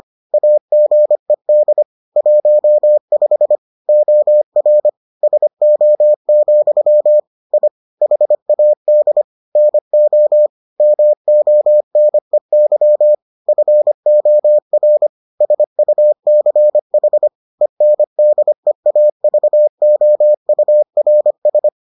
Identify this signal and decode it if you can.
Morse code, heard as a keyed tone, about 25 wpm, 600 Hz. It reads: AGED 15 OR SO, I HAD NO MONEY FOR SUCH ENDEAVOURS